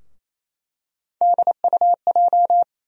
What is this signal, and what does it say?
Morse code, heard as a keyed tone, about 28 wpm, 700 Hz. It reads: DUJ
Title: USA Call Sign Postfixes - 3 Letters 28wpm